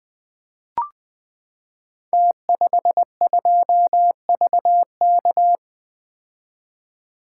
Morse code, heard as a keyed tone, about 20 wpm, 700 Hz. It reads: T52VK
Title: Code Groups - 5 Letter:Numbers - 20wpm